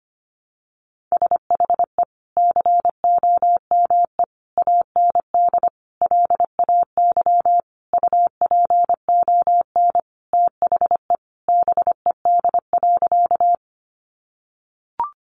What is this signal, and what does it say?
Morse code, heard as a keyed tone, about 25 wpm, 700 Hz. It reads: SHE COME AND LAY UPON THE BED.
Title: Sentences from Top 1100 Words 25wpm